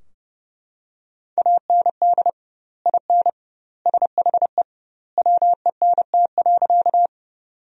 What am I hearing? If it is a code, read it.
Morse code, heard as a keyed tone, about 32 wpm, 700 Hz. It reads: AND IN SHE WENT.
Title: 2x Word-Spacing - Sentences from Top 200 Words - 30wpm